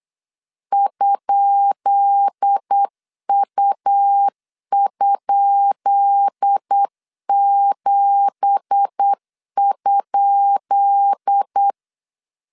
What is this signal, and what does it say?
Morse code, heard as a keyed tone, about 8 wpm, 800 Hz. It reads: ?U?7?
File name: LEZIONI / 10 SIGNS / 10B INT 8wpm 3534